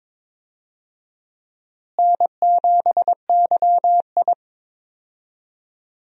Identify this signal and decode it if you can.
Morse code, heard as a keyed tone, about 22 wpm, 700 Hz. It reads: N7YI
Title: USA Call Signs - 4 Characters - Repeated 2x - 22wpm